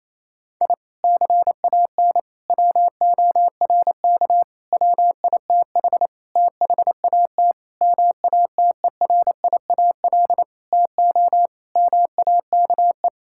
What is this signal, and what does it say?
Morse code, heard as a keyed tone, about 28 wpm, 700 Hz. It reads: I CAN WORK WITH THAT MATERIAL TO MAKE
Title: Sentences from Top 1000 Words - Review Part 5 - 28wpm